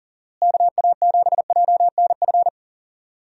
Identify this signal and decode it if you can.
Morse code, heard as a keyed tone, about 40 wpm, 700 Hz. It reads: KA7JNF
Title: USA Call Signs - 6 Characters - Repeated 2x - 40wpm